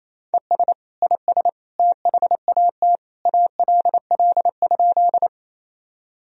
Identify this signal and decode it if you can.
Morse code, heard as a keyed tone, about 28 wpm, 700 Hz. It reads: ES IS THAT ALL?
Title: Sentences from Top 100 Words and Abbreviations 28wpm